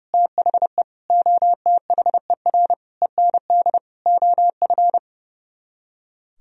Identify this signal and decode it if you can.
Morse code, heard as a keyed tone, about 40 wpm, 700 Hz. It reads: THE OTHER END OF